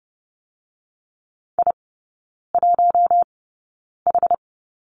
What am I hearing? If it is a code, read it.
Morse code, heard as a keyed tone, about 30 wpm, spaced as for 9 wpm, 700 Hz. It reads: I1H
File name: Lesson 17e - Letter u - Three Characters 30wpm